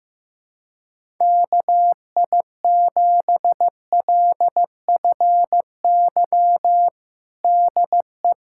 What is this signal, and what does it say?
Morse code, heard as a keyed tone, about 15 wpm, 700 Hz. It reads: KI7LFY DE
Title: USA Callsign DE Callsign 15wpm